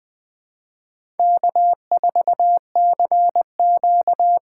Morse code, heard as a keyed tone, about 21 wpm, 700 Hz. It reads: K4CQ
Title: USA Call Signs - 4 Characters - 2x Character Spacing - 20wpm